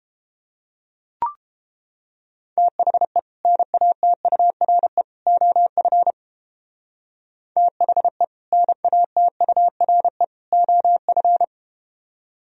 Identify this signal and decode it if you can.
Morse code, heard as a keyed tone, about 30 wpm, 700 Hz. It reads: THE NATURE OF THE NATURE OF